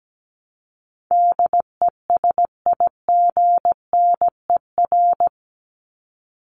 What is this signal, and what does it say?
Morse code, heard as a keyed tone, about 17 wpm, 700 Hz. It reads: DESIGNER
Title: Top 1501-1750 Words 17wpm